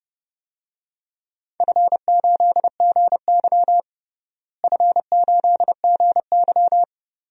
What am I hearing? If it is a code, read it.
Morse code, heard as a keyed tone, about 32 wpm, 700 Hz. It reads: F8GY F8GY
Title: DXCC-100 Call Signs - 4 Characters - Repeated 2x - 30wpm